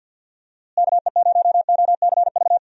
Morse code, heard as a keyed tone, about 47 wpm, 700 Hz. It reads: KE0OCV
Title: USA Call Signs - 6 Characters - Repeated 2x - 50wpm